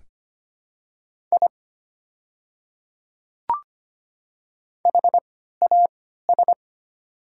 Morse code, heard as a keyed tone, about 25 wpm, spaced as for 14 wpm, 690 Hz. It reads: I HAS
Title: 3x Character Spacing - Top 300 Words - 25wpm